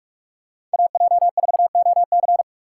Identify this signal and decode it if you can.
Morse code, heard as a keyed tone, about 46 wpm, 700 Hz. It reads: AJ4OC
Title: USA Call Signs - 5 Characters - 3x Character Spacing - 45wpm